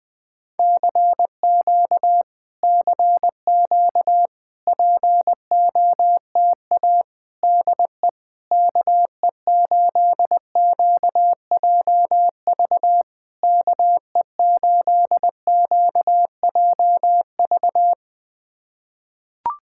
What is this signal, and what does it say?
Morse code, heard as a keyed tone, about 20 wpm, 690 Hz. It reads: CQ CQ POTA DE KE8QJV KE8QJV
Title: POTA QSO Practice - 20wpm